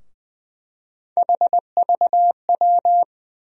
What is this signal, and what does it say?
Morse code, heard as a keyed tone, about 21 wpm, 700 Hz. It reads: HVW
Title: USA Call Sign Postfixes - 3 Letters 20wpm